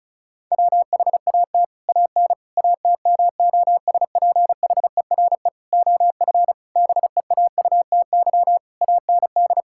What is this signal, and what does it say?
Morse code, heard as a keyed tone, about 34 wpm, 700 Hz. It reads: WHAT AN ATMOSPHERE OF BEAUTY AND
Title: Sentences from Top 1600 Words 35wpm